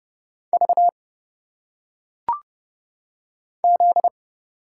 Morse code, heard as a keyed tone, about 30 wpm, 700 Hz. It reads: V Z
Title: Single Letters - Phonetic Pronunciation 30wpm